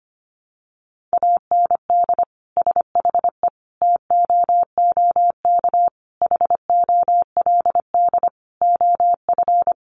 Morse code, heard as a keyed tone, about 25 wpm, 700 Hz. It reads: AND SHE TOOK HOLD OF
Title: Sentences from Top 1200 Words 25wpm